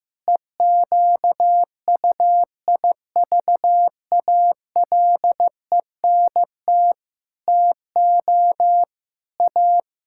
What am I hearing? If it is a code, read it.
Morse code, heard as a keyed tone, about 15 wpm, 700 Hz. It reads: EQUIVALENT TO A
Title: Sentences from Top 1600 Words 15wpm